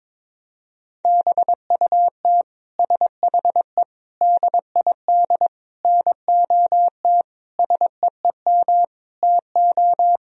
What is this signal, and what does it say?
Morse code, heard as a keyed tone, about 22 wpm, 700 Hz. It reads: BUT SHE DID NOT SEEM TO
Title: Sentences from Top 1000 Words - Review Part 1 - 22wpm